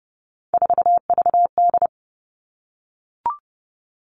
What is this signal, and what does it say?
Morse code, heard as a keyed tone, about 30 wpm, 700 Hz. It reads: 4VD
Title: All Cut Numbers - 3-Digit 30wpm